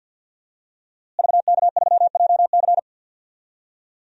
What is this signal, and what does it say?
Morse code, heard as a keyed tone, about 48 wpm, 700 Hz. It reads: VK3JC
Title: DXCC-100 Call Signs - 5 Characters - Repeated 2x - 50wpm